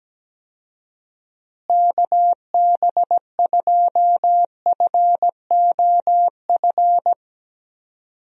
Morse code, heard as a keyed tone, about 17 wpm, 700 Hz. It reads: KB2FOF
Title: USA Call Signs - 6 Characters - Repeated 2x - 17wpm